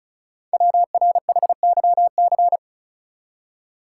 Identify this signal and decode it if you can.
Morse code, heard as a keyed tone, about 36 wpm, 690 Hz. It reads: WRHYC
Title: Code Groups - 5 Letters - 35wpm